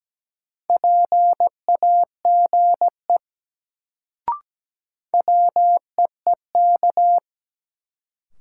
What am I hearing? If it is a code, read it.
Morse code, heard as a keyed tone, about 17 wpm, 700 Hz. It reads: PAGE WEEK
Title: Sets of 5 Words Encore - New Word Drill - 17wpm